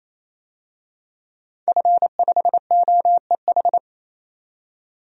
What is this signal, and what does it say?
Morse code, heard as a keyed tone, about 28 wpm, 700 Hz. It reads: F5OEH
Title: DXCC-100 Call Signs - 5 Characters - 28wpm